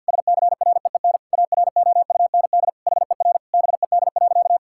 Morse code, heard as a keyed tone, about 52 wpm, 700 Hz. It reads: SCREEN AROUND HER BED.
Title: Sentences from Top 1000 Words - Review Part 6 - 50wpm